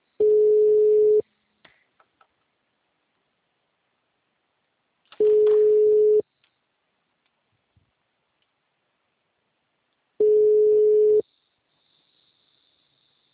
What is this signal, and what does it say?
Ringback tone (Europe, 425 Hz).
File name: out- -2001- - - .182